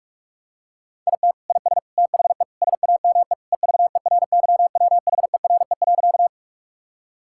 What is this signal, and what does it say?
Morse code, heard as a keyed tone, about 48 wpm, 700 Hz. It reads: IT IS THE SAME EVERYWHERE.
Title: Sentences from Top 1600 Words 45wpm